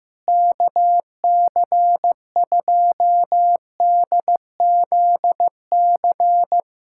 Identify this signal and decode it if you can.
Morse code, heard as a keyed tone, about 15 wpm, 700 Hz. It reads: KC2DZC
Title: USA Callsign DE Callsign 15wpm